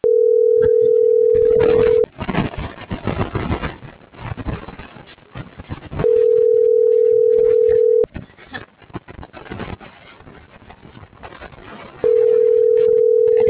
Ringback tone (North America), 440 + 480 Hz.